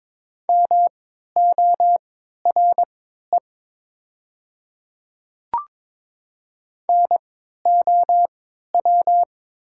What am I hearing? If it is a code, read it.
Morse code, heard as a keyed tone, about 22 wpm, spaced as for 12 wpm, 690 Hz. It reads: MORE NOW
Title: 3x Character Spacing - Top 500 Words - 22wpm